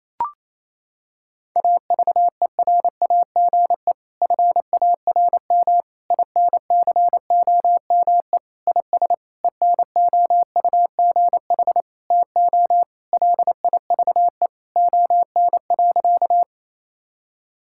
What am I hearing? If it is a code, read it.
Morse code, heard as a keyed tone, about 28 wpm, 700 Hz. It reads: AVERAGE FARM INCOME IS ENOUGH TO LIVE ON.
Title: Sentences from Top 1000 Words - Review Part 6 - 28wpm